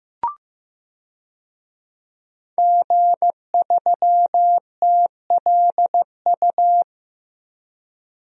Morse code, heard as a keyed tone, about 15 wpm, 700 Hz. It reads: G3TLU
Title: DXCC-100 Call Signs - 5 Characters - Repeated 3x - 15wpm